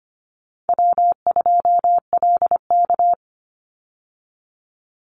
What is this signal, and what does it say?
Morse code, heard as a keyed tone, about 25 wpm, 700 Hz. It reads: W2LK
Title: Easy Call Signs Repeated 2x - 25wpm